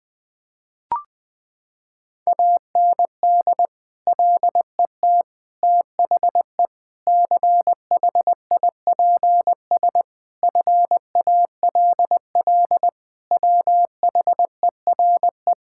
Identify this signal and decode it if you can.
Morse code, heard as a keyed tone, about 20 wpm, 700 Hz. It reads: AND LET THE CHIPS FALL WHERE
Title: Sets of 8 Words - 20wpm